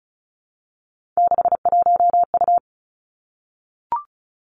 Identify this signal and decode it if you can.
Morse code, heard as a keyed tone, about 36 wpm, 700 Hz. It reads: 61U
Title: Three Challenging Letter-Number Combinations 35wpm